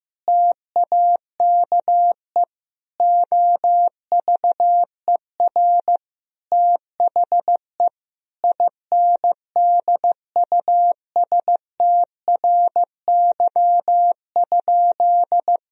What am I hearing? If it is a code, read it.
Morse code, heard as a keyed tone, about 15 wpm, 700 Hz. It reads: TAKE OVER THE INDUSTRY?
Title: Sentences from Top 1000 Words - Review Part 4 - 15wpm